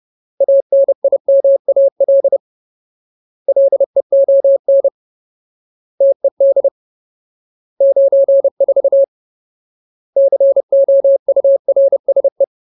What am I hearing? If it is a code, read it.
Morse code, heard as a keyed tone, about 30 wpm, 550 Hz. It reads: ANIMAL LEON TED 94 COURSE
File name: Ultimate Word List / 30WPM Ultimate